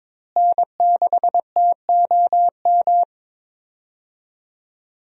Morse code, heard as a keyed tone, about 22 wpm, 700 Hz. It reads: N6TOM
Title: USA Call Signs - 5 Characters - Repeated 2x - 22wpm